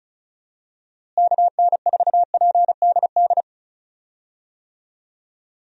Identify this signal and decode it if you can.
Morse code, heard as a keyed tone, about 34 wpm, 700 Hz. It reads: KN4PDD
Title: USA Call Signs - 6 Characters - Repeated 2x - 35wpm